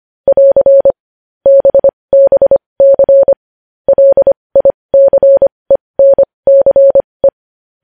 Morse code, heard as a keyed tone, about 25 wpm, spaced as for 20 wpm, 550 Hz. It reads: <AR> BBC LICENCE
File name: podcast / News headlines-1x-25x20